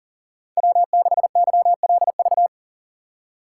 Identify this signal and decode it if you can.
Morse code, heard as a keyed tone, about 40 wpm, 700 Hz. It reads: W6YLV
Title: USA Call Signs - 5 Characters - 3x Character Spacing - 40wpm